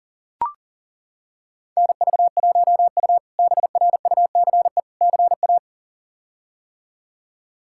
Morse code, heard as a keyed tone, about 40 wpm, 700 Hz. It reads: NV1U BRUCE CA
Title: K1USN SST QSO Practice - 40wpm